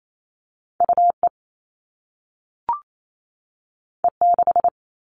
Morse code, heard as a keyed tone, about 28 wpm, 700 Hz. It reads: UE E6
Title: All Cut Numbers - 2-Digit 28wpm